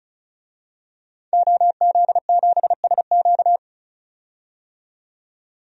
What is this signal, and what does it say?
Morse code, heard as a keyed tone, about 36 wpm, 700 Hz. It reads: OZ7SQ
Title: DXCC-100 Call Signs - 5 Characters - 35wpm